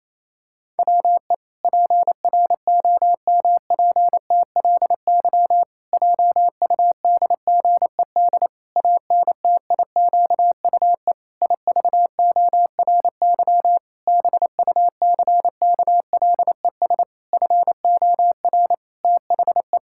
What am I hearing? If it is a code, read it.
Morse code, heard as a keyed tone, about 28 wpm, 700 Hz. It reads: WE PROMPTLY JUDGED ANTIQUE IVORY BUCKLES FOR THE
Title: Pangram Challenge v2 - 28wpm